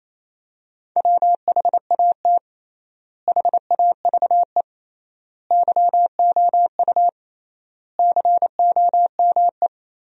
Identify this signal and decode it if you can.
Morse code, heard as a keyed tone, about 28 wpm, 700 Hz. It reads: WHAT HAVE YOU COME
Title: 3x Word-Spacing - Sentences from Top 100 Words - 28wpm